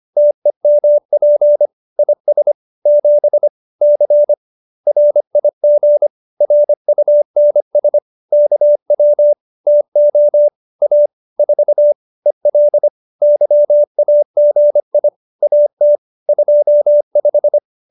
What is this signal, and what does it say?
Morse code, heard as a keyed tone, about 25 wpm, 590 Hz. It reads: TEMP IS 7 C RIG RUNS KW TO A 4 EL YAGI AT 25
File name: QSO305 25